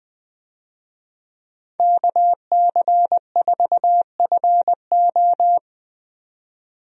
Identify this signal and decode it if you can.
Morse code, heard as a keyed tone, about 20 wpm, 700 Hz. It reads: KC4FO
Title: USA Call Signs - 5 Characters - 20wpm